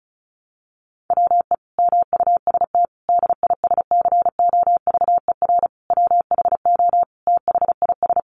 Morse code, heard as a keyed tone, about 34 wpm, 700 Hz. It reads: WE MUST DISCOVER WHO THIS
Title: Sentences from Top 1100 Words 35wpm